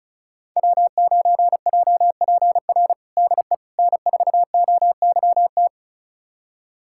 Morse code, heard as a keyed tone, about 36 wpm, 700 Hz. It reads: W9JPR DE N4OYT
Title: USA Callsign DE Callsign - 5 Characters - 35wpm